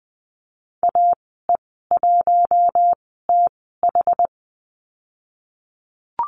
Morse code, heard as a keyed tone, about 20 wpm, spaced as for 15 wpm, 700 Hz. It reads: AE1TH
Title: 088 Lesson 15g - Number 5 - Call Signs 20wpm